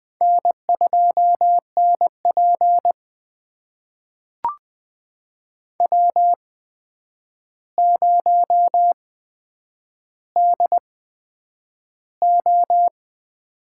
Morse code, heard as a keyed tone, about 20 wpm, spaced as for 9 wpm, 700 Hz. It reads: N2NP W0DO
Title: USA Call Signs - 4 Characters - 8x Character Spacing - 20wpm